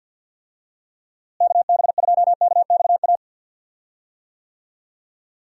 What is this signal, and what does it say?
Morse code, heard as a keyed tone, about 48 wpm, 700 Hz. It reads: KB2KXA
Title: USA Call Signs - 6 Characters - Repeated 3x - 50wpm